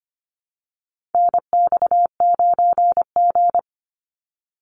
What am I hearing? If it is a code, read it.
Morse code, heard as a keyed tone, about 25 wpm, 700 Hz. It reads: NX9G
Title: Call Signs - 25wpm